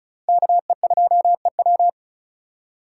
Morse code, heard as a keyed tone, about 34 wpm, 700 Hz. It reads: KE2EW
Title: USA Call Signs - 5 Characters - 4x Character Spacing - 35wpm